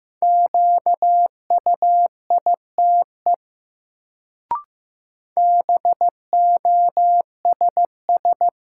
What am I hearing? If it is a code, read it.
Morse code, heard as a keyed tone, about 15 wpm, 700 Hz. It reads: QUITE BOSS
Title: Sentences from Top 700 Words - New Word Drill - 15wpm